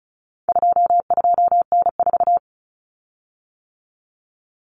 Morse code, heard as a keyed tone, about 34 wpm, 700 Hz. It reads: 22N4
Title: Common Cut Numbers - 4-Digit 35wpm